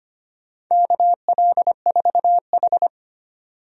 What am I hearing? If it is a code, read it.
Morse code, heard as a keyed tone, about 25 wpm, 700 Hz. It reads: KL4H
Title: USA Call Signs - 4 Characters - 25wpm